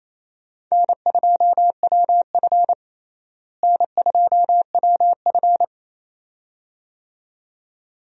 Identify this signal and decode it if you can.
Morse code, heard as a keyed tone, about 27 wpm, 700 Hz. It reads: N2WF N2WF
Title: USA Call Signs - 4 Characters - Repeated 2x - 28wpm